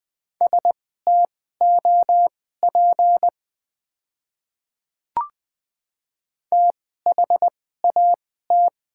Morse code, heard as a keyed tone, about 20 wpm, spaced as for 15 wpm, 690 Hz. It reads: STOP THAT
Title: 2x Character Spacing - Top 500 Words - 20wpm